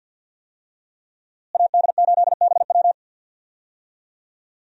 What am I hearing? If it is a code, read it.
Morse code, heard as a keyed tone, about 52 wpm, 700 Hz. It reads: AD8BW
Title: USA Call Signs - 5 Characters - Repeated 2x - 50wpm